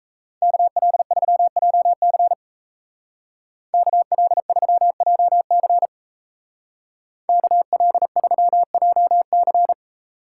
Morse code, heard as a keyed tone, about 35 wpm, 700 Hz. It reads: KL3JC KL3JC KL3JC
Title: Speed Racing - USA 2x2 Call Signs 28wpm